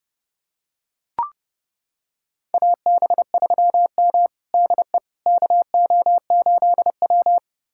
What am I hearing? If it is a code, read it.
Morse code, heard as a keyed tone, about 30 wpm, 700 Hz. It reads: AB3M DE KO8W
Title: USA Callsign DE Callsign - 4 Characters - 30wpm